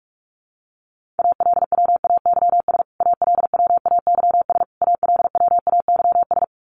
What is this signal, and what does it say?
Morse code, heard as a keyed tone, about 44 wpm, 700 Hz. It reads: ALWAYS ALWAYS ALWAYS
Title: Top 300 Words Repeated 45wpm